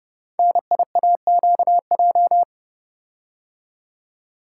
Morse code, heard as a keyed tone, about 28 wpm, 700 Hz. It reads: NIAQJ
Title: Code Groups - 5 Letters - 30wpm